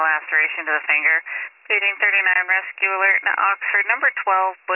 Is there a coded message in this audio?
Yes: Emergency Alert System (SAME) tones from 1.0 to 4.2 s, heard as harsh buzzing bursts, with speech.